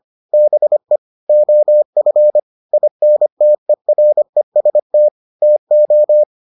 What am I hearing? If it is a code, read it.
Morse code, heard as a keyed tone, about 25 wpm, 600 Hz. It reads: BE OF INTEREST TO